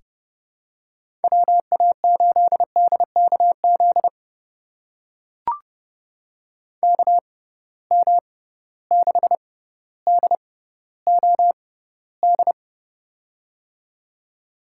Morse code, heard as a keyed tone, about 30 wpm, spaced as for 15 wpm, 700 Hz. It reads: WA8DKZ KM6DOD
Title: USA Call Signs - 6 Characters - 6x Character Spacing - 30wpm